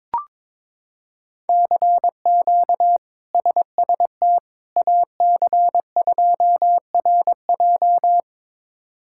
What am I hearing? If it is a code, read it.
Morse code, heard as a keyed tone, about 21 wpm, 700 Hz. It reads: CQ SST AC2RJ
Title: K1USN SST QSO Practice - 22wpm